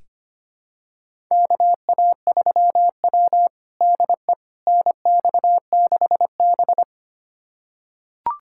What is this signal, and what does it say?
Morse code, heard as a keyed tone, about 25 wpm, 700 Hz. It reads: KA3W DE NX6B
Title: USA Callsign DE Callsign - 4 Characters - 25wpm